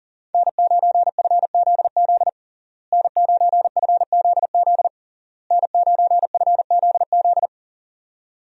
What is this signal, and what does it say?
Morse code, heard as a keyed tone, about 37 wpm, 700 Hz. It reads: N9FZZ N9FZZ N9FZZ
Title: USA Call Signs - 5 Characters - Repeated 3x - 40wpm